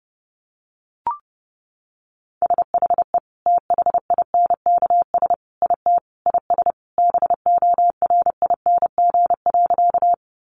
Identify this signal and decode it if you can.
Morse code, heard as a keyed tone, about 30 wpm, 700 Hz. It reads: SHE THINKS IT IS BORING.
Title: Easy Phrases 30wpm